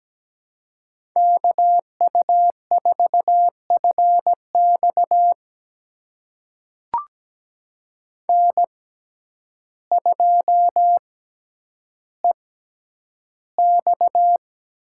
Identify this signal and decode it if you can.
Morse code, heard as a keyed tone, about 17 wpm, spaced as for 9 wpm, 700 Hz. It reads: KU4FX N2EX
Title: USA Call Signs - 6x Character Spacing - 17wpm